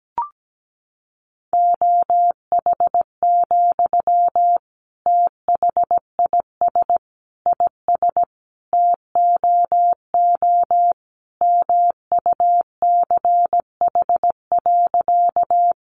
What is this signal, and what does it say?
Morse code, heard as a keyed tone, about 17 wpm, 700 Hz. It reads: OH, THIS IS TOO MUCH.
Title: Sentences from Top 500 Words 17wpm